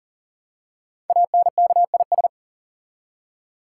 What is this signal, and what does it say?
Morse code, heard as a keyed tone, about 42 wpm, 690 Hz. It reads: ANKIS
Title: Code Groups - 5 Letters - 40wpm